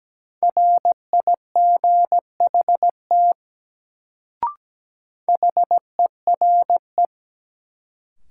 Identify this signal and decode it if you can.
Morse code, heard as a keyed tone, about 17 wpm, 700 Hz. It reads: RIGHT HERE
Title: Sentences from Top 200 Words - New Word Drill - 17wpm